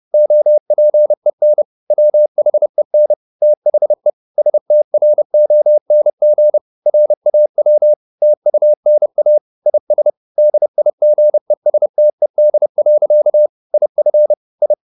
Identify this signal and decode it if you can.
Morse code, heard as a keyed tone, about 30 wpm, 600 Hz. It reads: OPEN WHEN THE STRONG RAW TUNA IS DIGESTED. IF I
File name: the old man and the sea / the old man and the sea 1 30wpm